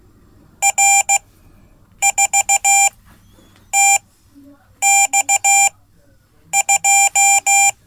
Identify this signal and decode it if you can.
Morse code, heard as a keyed tone, about 15 wpm, spaced as for 8 wpm, 790 Hz. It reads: R4TX2
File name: Code / 5min-c